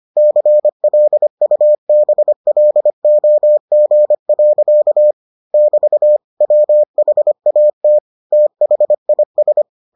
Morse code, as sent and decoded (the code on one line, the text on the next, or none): -.-. .-.. ..- -... .-.. --- --. .-.-.- / -...- / .-- .... .- - / - .... .. ...
CLUBLOG. <BT> WHAT THIS